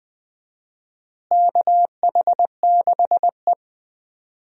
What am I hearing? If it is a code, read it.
Morse code, heard as a keyed tone, about 21 wpm, 700 Hz. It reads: KH6E
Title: DXCC-100 Call Signs - 4 Characters - Repeated 2x - 20wpm